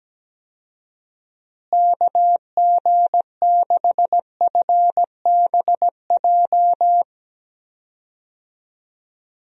Morse code, heard as a keyed tone, about 17 wpm, 700 Hz. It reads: KG6FBJ
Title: USA Call Signs - 6 Characters - Repeated 2x - 17wpm